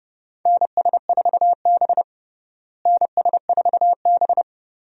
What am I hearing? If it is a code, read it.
Morse code, heard as a keyed tone, about 32 wpm, 700 Hz. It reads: NS4B NS4B
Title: USA Call Signs - 4 Characters - Repeated 3x - 30wpm